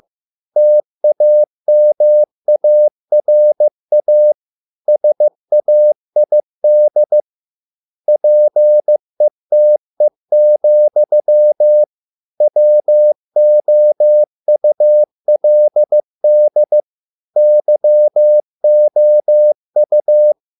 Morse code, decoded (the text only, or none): TAMARA SAID PETE, WOULD YOU